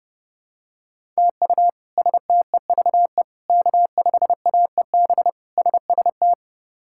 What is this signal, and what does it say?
Morse code, heard as a keyed tone, about 30 wpm, 700 Hz. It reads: TU STEVE K5AEB SST
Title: K1USN SST QSO Practice - 30wpm